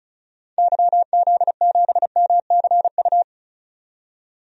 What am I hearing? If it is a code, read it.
Morse code, heard as a keyed tone, about 34 wpm, 700 Hz. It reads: YZ7MCU
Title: DXCC-100 Call Signs - 6 Characters - 35wpm